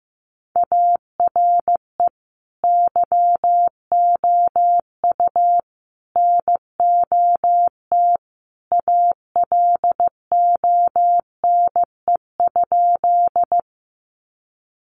Morse code, heard as a keyed tone, about 14 wpm, 700 Hz. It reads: ARE YOU NOT ALONE?
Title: Sentences from Top 1000 Words 15wpm